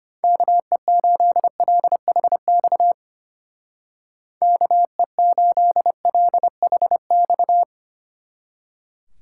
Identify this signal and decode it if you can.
Morse code, heard as a keyed tone, about 32 wpm, 700 Hz. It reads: KE8LHX KE8LHX